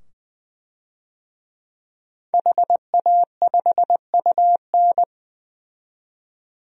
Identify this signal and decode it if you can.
Morse code, heard as a keyed tone, about 20 wpm, 700 Hz. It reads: HA5UN
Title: DXCC-100 Call Signs - 5 Characters - 20wpm